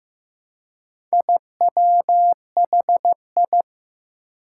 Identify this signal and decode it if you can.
Morse code, heard as a keyed tone, about 15 wpm, 700 Hz. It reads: IWHI
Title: Four Challenging Letter-Number Combinations 15wpm